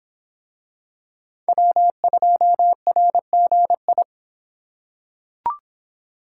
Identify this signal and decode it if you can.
Morse code, heard as a keyed tone, about 26 wpm, 700 Hz. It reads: W2RGI